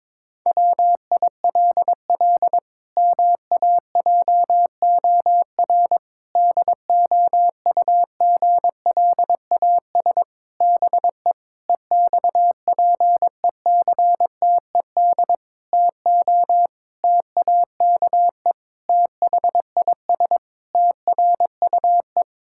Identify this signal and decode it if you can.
Morse code, heard as a keyed tone, about 22 wpm, 700 Hz. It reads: WILL MAJOR DOUGLAS BE EXPECTED TO TAKE THIS TRUE
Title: Pangram Challenge v2 - 22wpm